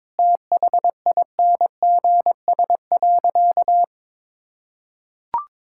Morse code, heard as a keyed tone, about 22 wpm, 700 Hz. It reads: THINGS.
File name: practice / 22wpm / Sentences with 8-Word Phrases - 22wpm